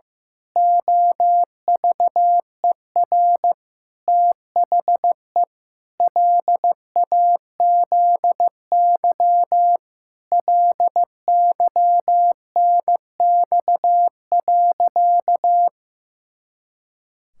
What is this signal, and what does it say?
Morse code, heard as a keyed tone, about 15 wpm, 700 Hz. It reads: OVER THE LAZY LYNX.
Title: Pangram Challenge v2 - 15wpm